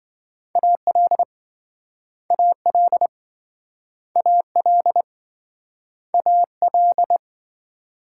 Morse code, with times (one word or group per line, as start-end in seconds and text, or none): 0.6-1.2 AL
2.3-3.1 AL
4.2-5.0 AL
6.1-7.2 AL